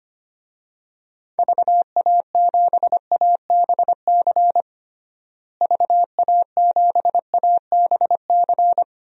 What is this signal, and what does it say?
Morse code, heard as a keyed tone, about 25 wpm, 700 Hz. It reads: VA7ABC VA7ABC
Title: DXCC-100 Call Signs - 6 Characters - Repeated 2x - 25wpm